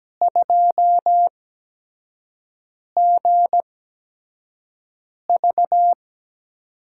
Morse code, heard as a keyed tone, about 17 wpm, spaced as for 5 wpm, 700 Hz. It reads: 2GV
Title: USA Call Signs - 5 Characters - 8x Character Spacing - 17wpm